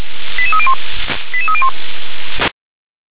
Five-tone selective call (ZVEI), heard as quick runs of five tones, tones of about 70 ms. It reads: 88301 88301
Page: SDR/ZVEI